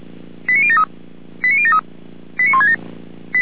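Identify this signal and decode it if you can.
Five-tone selective call (ZVEI), heard as quick runs of five tones, tones of about 80 ms. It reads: 89073 89073 89167
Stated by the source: ZVEI five-tone call 89022, received on SDR by feuerwehr-orlamuende (fire brigade Orlamünde).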